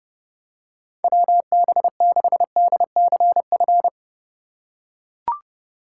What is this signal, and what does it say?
Morse code, heard as a keyed tone, about 32 wpm, 700 Hz. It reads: WB6DCF
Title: USA Call Signs - 6 Characters - 6x Character Spacing - 30wpm